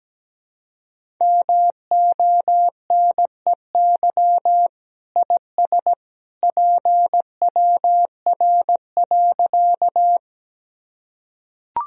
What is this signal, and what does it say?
Morse code, heard as a keyed tone, about 17 wpm, 700 Hz. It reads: MONEY IS PWR.
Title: Sentences from Top 500 Words and Abbreviations 17wpm